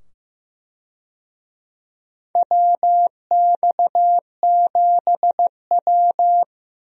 Morse code, heard as a keyed tone, about 15 wpm, 700 Hz. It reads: WX7W
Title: USA Call Signs - 4 Characters - 15wpm